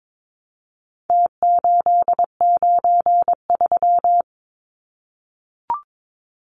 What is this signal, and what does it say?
Morse code, heard as a keyed tone, about 22 wpm, 700 Hz. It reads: T893
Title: Numbers - Contest Style - 4-Digit 22wpm